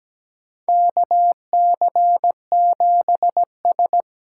Morse code, heard as a keyed tone, about 17 wpm, 700 Hz. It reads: KC7S
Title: USA Call Signs - 6 Characters - 8x Character Spacing - 17wpm